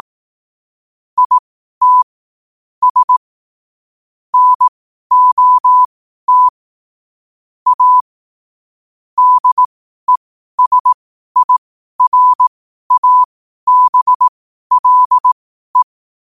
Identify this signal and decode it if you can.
Morse code, heard as a keyed tone, about 18 wpm, spaced as for 12 wpm, 1000 Hz. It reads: IT S NOT A DESIRABLE
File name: qotdfiles / 12wpm